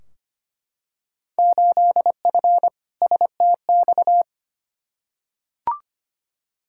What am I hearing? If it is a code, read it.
Morse code, heard as a keyed tone, about 25 wpm, 700 Hz. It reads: 8F STX
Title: Field Day Exchanges Encore - 25wpm